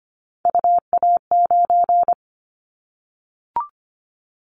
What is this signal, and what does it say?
Morse code, heard as a keyed tone, about 25 wpm, 700 Hz. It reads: UA9
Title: Callsign Prefixes for Top 25 DXCC - 25wpm